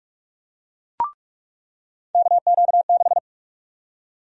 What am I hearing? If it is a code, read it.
Morse code, heard as a keyed tone, about 48 wpm, 700 Hz. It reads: KQ6
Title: USA Call Sign Prefixes - 45wpm